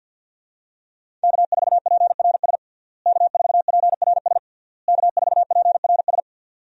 Morse code, heard as a keyed tone, about 50 wpm, 700 Hz. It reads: K4PRS K4PRS K4PRS
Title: USA Call Signs - 5 Characters - Repeated 3x - 50wpm